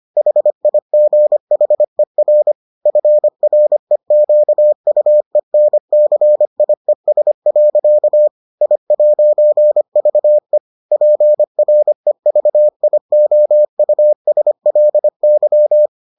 Morse code, heard as a keyed tone, about 25 wpm, 600 Hz. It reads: HIGHER FREQUENCIES. I'VE PREVIOUSLY